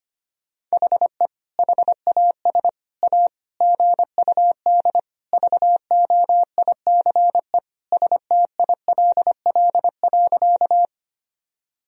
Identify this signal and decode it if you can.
Morse code, heard as a keyed tone, about 25 wpm, 700 Hz. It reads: HE HAS A GUD VOICE STILL.
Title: Sentences from Top 500 Words and Abbreviations 25wpm